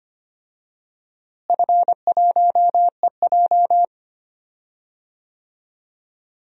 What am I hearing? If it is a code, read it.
Morse code, heard as a keyed tone, about 25 wpm, 700 Hz. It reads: F1EJ
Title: DXCC-100 Call Signs - 4 Characters - Repeated 3x - 25wpm